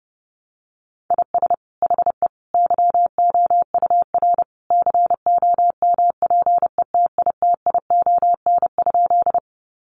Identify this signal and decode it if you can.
Morse code, heard as a keyed tone, about 30 wpm, 700 Hz. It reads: IS HE YOUR COMPETITION?
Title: Sentences from Top 1000 Words 30wpm